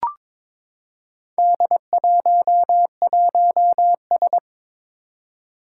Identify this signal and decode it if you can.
Morse code, heard as a keyed tone, about 23 wpm, 700 Hz. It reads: D11S
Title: Four Challenging Letter-Number Combinations 22wpm